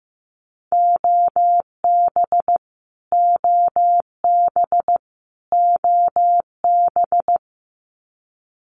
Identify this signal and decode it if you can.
Morse code, heard as a keyed tone, about 15 wpm, 700 Hz. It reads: OB OB OB
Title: Abbreviations Repeated 3x - 15wpm